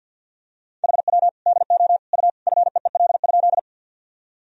Morse code, heard as a keyed tone, about 46 wpm, 700 Hz. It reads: HW DO U FEEL?
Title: Sentences from Top 400 Words and Abbreviations 50wpm